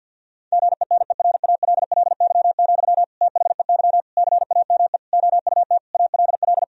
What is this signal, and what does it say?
Morse code, heard as a keyed tone, about 50 wpm, 700 Hz. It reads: GENERALLY, THEY CAME OUT ALL
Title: Sentences from Top 1000 Words - Review Part 4 - 50wpm